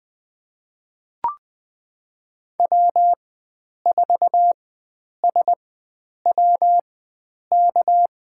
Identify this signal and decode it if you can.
Morse code, heard as a keyed tone, about 20 wpm, spaced as for 9 wpm, 700 Hz. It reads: W4SWK
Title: USA Call Signs - 5 Characters - 4x Character Spacing - 20wpm